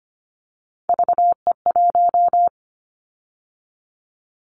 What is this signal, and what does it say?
Morse code, heard as a keyed tone, about 25 wpm, 700 Hz. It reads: VE1
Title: Callsign Prefixes for Top 100 DXCC - 25wpm